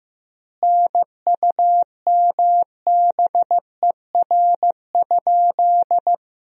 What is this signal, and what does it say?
Morse code, heard as a keyed tone, about 15 wpm, 700 Hz. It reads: NUMBER?
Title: 3x Word-Spacing - Sentences from Top 100 Words - 15wpm